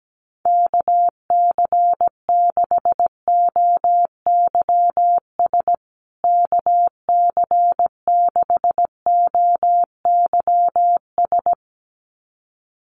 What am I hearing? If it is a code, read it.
Morse code, heard as a keyed tone, about 17 wpm, 700 Hz. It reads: KC6OYS KC6OYS
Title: USA Call Signs Repeated 2x 17wpm